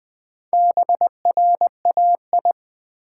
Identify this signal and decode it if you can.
Morse code, heard as a keyed tone, about 21 wpm, 700 Hz. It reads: BRAI
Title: Build A Word - 1 Letter At A Time - 5-Letter Words - 20wpm